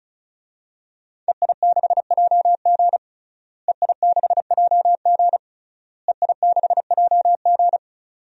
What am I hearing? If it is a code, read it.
Morse code, heard as a keyed tone, about 36 wpm, 700 Hz. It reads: EI6JG EI6JG EI6JG
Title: DXCC-100 Call Signs - 5 Characters - Repeated 3x - 35wpm